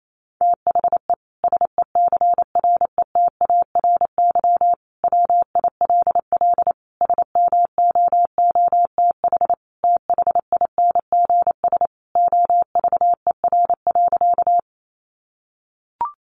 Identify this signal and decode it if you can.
Morse code, heard as a keyed tone, about 28 wpm, 700 Hz. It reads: THE SECRETARY WILL SMOOTH THINGS OVER.
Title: Sentences from Top 1400 Words 28wpm